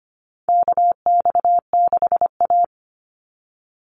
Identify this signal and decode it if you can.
Morse code, heard as a keyed tone, about 25 wpm, 700 Hz. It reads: KX6A
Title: NA SOTA Chaser Callsigns 25wpm